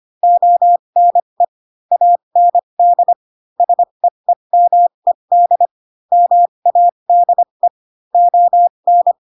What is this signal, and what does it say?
Morse code, heard as a keyed tone, about 25 wpm, spaced as for 22 wpm, 700 Hz. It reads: ONE AND SEEMED MADE ON
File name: Pinocchio / Pinocchio ch3 18 25 ik1hge